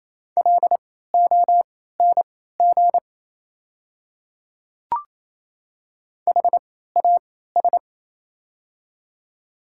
Morse code, heard as a keyed tone, about 28 wpm, spaced as for 16 wpm, 690 Hz. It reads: LONG HAS
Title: 3x Character Spacing - Top 500 Words - 28wpm